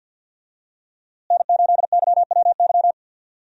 Morse code, heard as a keyed tone, about 49 wpm, 700 Hz. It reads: N8YWY
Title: USA Call Signs - 5 Characters - Repeated 3x - 50wpm